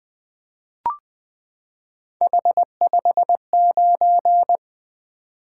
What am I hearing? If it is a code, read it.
Morse code, heard as a keyed tone, about 20 wpm, 700 Hz. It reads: H59
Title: Three Letter-Number Combinations - 20wpm